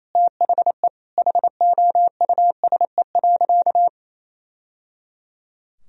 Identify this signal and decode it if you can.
Morse code, heard as a keyed tone, about 28 wpm, 700 Hz. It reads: THE HOUSE.
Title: Sentences from Top 1000 Words - Review Part 1 - 28wpm